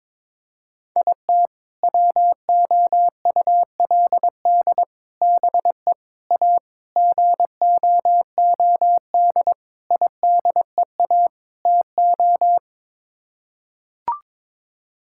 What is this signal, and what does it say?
Morse code, heard as a keyed tone, about 21 wpm, 700 Hz. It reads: IT WOULD BE A GOOD IDEA TO
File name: practice / 22wpm / Sets of 7 Words - 22wpm